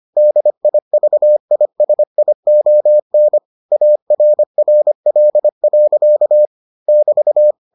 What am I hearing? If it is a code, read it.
Morse code, heard as a keyed tone, about 25 wpm, 600 Hz. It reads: DIVISION ARRL. <BT>